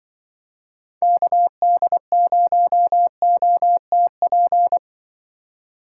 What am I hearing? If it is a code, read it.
Morse code, heard as a keyed tone, about 24 wpm, 700 Hz. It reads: KD0OTP
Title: Speed Racing - USA 2x3 Call Signs 20wpm